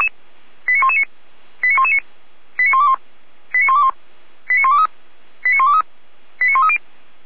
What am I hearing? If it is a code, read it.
Five-tone selective call (ZVEI), heard as quick runs of five tones, tones of about 70 ms. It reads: 89119 89119 89121 89121 89123 89123 89130